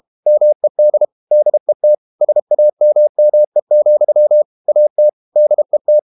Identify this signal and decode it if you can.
Morse code, heard as a keyed tone, about 32 wpm, 600 Hz. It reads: MED DET SAMME, AT DET
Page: Ferie 32 wpm